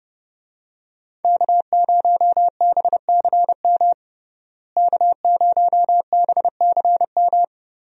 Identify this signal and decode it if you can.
Morse code, heard as a keyed tone, about 30 wpm, 700 Hz. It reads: K0BCM K0BCM
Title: USA Call Signs - 5 Characters - Repeated 2x - 30wpm